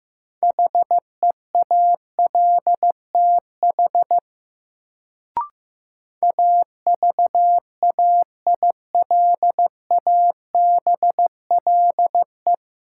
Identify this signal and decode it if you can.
Morse code, heard as a keyed tone, about 15 wpm, 700 Hz. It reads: HEALTH AVAILABLE
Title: Sentences from Top 600 Words - New Word Drill - 15wpm